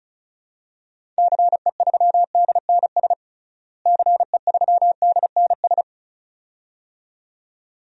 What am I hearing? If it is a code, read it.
Morse code, heard as a keyed tone, about 36 wpm, 700 Hz. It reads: CE3DNS CE3DNS
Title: DXCC-100 Call Signs - 6 Characters - Repeated 2x - 35wpm